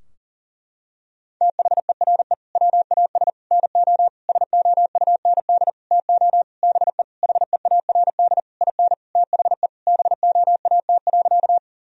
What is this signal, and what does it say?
Morse code, heard as a keyed tone, about 40 wpm, 700 Hz. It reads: THERE WAS NO SOUND TO BE HEARD IN THE BOAT.
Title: Sentences from Top 1000 Words - Review Part 3 - 40wpm